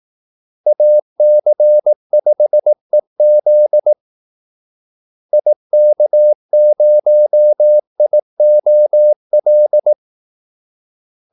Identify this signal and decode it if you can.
Morse code, heard as a keyed tone, about 18 wpm, 600 Hz. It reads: AC5EZ IK0IOL
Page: Kaldesignaler 18 wpm